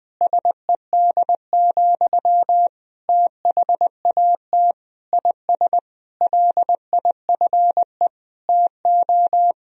Morse code, heard as a keyed tone, about 20 wpm, 700 Hz. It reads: SED, THAT IS LIFE TO
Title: Sentences from Top 500 Words and Abbreviations 20wpm